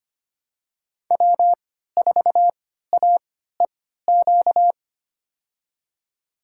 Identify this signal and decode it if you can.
Morse code, heard as a keyed tone, about 25 wpm, spaced as for 14 wpm, 700 Hz. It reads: W4AEQ
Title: USA Call Signs - 3x Character Spacing - 25wpm